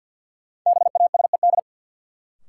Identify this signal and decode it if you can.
Morse code, heard as a keyed tone, about 51 wpm, 700 Hz. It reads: BASED
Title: Sets of 2 Words - New Word Drill - 50wpm